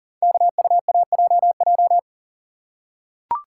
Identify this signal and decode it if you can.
Morse code, heard as a keyed tone, about 42 wpm, 700 Hz. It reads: KUAJJ
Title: Code Groups - 5 Letters - 40wpm